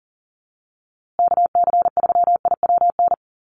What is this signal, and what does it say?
Morse code, heard as a keyed tone, about 40 wpm, 700 Hz. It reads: KC3IWN
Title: USA Call Signs Repeated 2x 40wpm